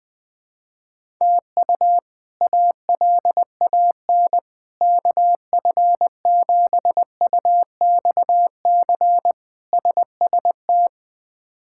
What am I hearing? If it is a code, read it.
Morse code, heard as a keyed tone, about 20 wpm, 700 Hz. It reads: TU ALAN KF7UXC SST
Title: K1USN SST QSO Practice - 20wpm